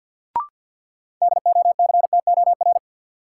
Morse code, heard as a keyed tone, about 53 wpm, 700 Hz. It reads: DOCTOR
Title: Sentences from Top 1500 Words - New Word Drill - 50wpm